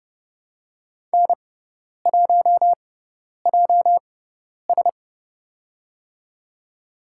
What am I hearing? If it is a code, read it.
Morse code, heard as a keyed tone, about 30 wpm, spaced as for 10 wpm, 700 Hz. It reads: N1JS
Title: USA Call Signs - 4 Characters - 6x Character Spacing - 30wpm